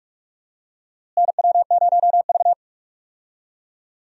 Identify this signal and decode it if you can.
Morse code, heard as a keyed tone, about 47 wpm, 700 Hz. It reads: NW0V
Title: USA Call Signs - 4 Characters - Repeated 2x - 45wpm